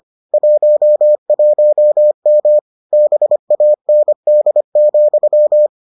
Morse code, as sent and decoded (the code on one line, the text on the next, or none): .---- .---- -- / -... .- -. -.. --..--
11M BAND,